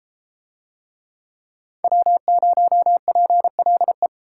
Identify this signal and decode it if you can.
Morse code, heard as a keyed tone, about 30 wpm, 700 Hz. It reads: W0PLE